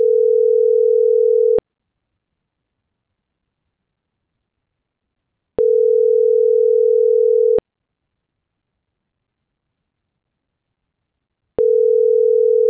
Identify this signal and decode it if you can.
Ringback tone (North America), 440 + 480 Hz.